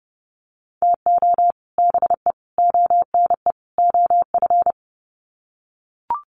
Morse code, heard as a keyed tone, about 30 wpm, 700 Hz. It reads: TO BE ONE OF
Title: Sets of 4 Words 30wpm